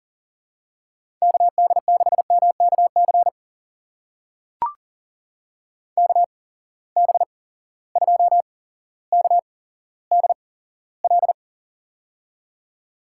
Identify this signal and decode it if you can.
Morse code, heard as a keyed tone, about 40 wpm, spaced as for 17 wpm, 700 Hz. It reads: KD6MKC KB2KDL
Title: USA Call Signs - 6 Characters - 8x Character Spacing - 40wpm